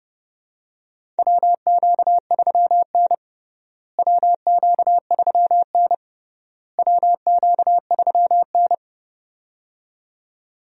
Morse code, heard as a keyed tone, about 32 wpm, 700 Hz. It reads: WQ3N WQ3N WQ3N
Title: USA Call Signs - 4 Characters - Repeated 3x - 30wpm